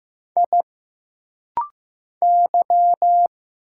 Morse code, heard as a keyed tone, about 15 wpm, 700 Hz. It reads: I Y
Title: Single Letters - Rapid-Fire 15wpm